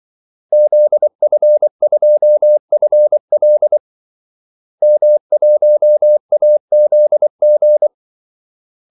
Morse code, heard as a keyed tone, about 24 wpm, 600 Hz. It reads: ZF2FL M1AZG
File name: Callsigns-24wpm